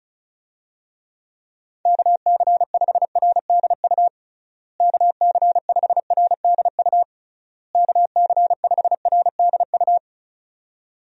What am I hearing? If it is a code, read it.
Morse code, heard as a keyed tone, about 34 wpm, 700 Hz. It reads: KC5RDU KC5RDU KC5RDU
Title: USA Call Signs - 6 Characters - Repeated 3x - 35wpm